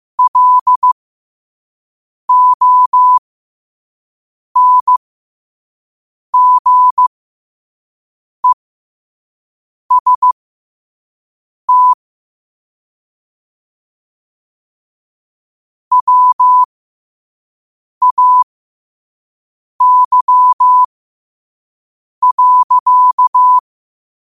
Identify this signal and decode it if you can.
Morse code, heard as a keyed tone, about 15 wpm, spaced as for 5 wpm, 1000 Hz. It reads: LONGEST WAY.
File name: 5wpm